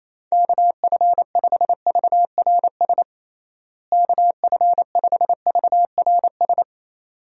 Morse code, heard as a keyed tone, about 28 wpm, 700 Hz. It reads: KF5VRS KF5VRS
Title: USA Call Signs - 6 Characters - Repeated 2x - 28wpm